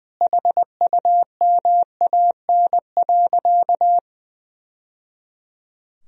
Morse code, heard as a keyed tone, about 21 wpm, 700 Hz. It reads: HUMAN.
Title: Sentences from Top 1000 Words - Review Part 3 - 20wpm